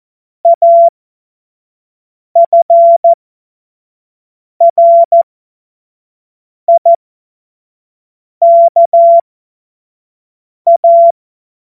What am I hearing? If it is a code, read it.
Morse code, heard as a keyed tone, about 14 wpm, spaced as for 5 wpm, 680 Hz. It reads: AFRIKA
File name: Morsekurs / Witze / Witz 27. September 2022 25 F70